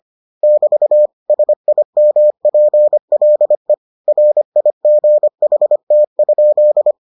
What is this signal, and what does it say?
Morse code, heard as a keyed tone, about 25 wpm, 600 Hz. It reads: <BT> SIMPLE RIGHT?